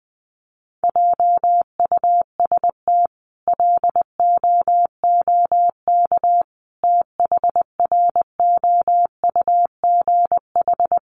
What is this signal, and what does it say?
Morse code, heard as a keyed tone, about 20 wpm, 700 Hz. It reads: JUST LOOK THROUGH
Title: Sentences from Top 200 Words 20wpm